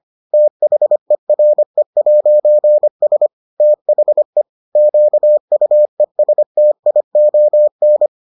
Morse code, heard as a keyed tone, about 25 wpm, 600 Hz. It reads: THERE'S THE QUESTION